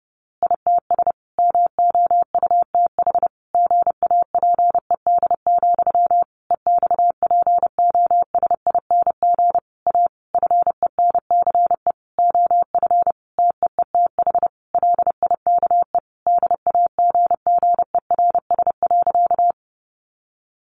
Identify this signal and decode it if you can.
Morse code, heard as a keyed tone, about 30 wpm, 700 Hz. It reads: ITS MOUTH GAPED, EXPOSING A FENCE OF TEETH LIKE DAGGERS.
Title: A Sound Of Thunder 30wpm